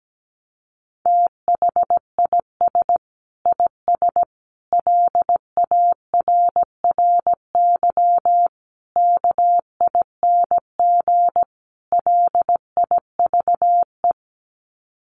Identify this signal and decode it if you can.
Morse code, heard as a keyed tone, about 17 wpm, 700 Hz. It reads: THIS IS LARRY KING LIVE
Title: Sets of 5 Words Encore2 17wpm